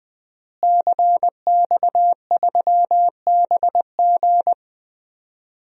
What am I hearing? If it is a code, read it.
Morse code, heard as a keyed tone, about 21 wpm, 700 Hz. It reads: CX3BG
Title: DXCC-100 Call Signs - 5 Characters - Repeated 2x - 20wpm